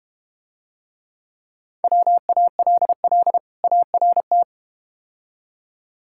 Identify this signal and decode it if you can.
Morse code, heard as a keyed tone, about 32 wpm, 700 Hz. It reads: WALL ART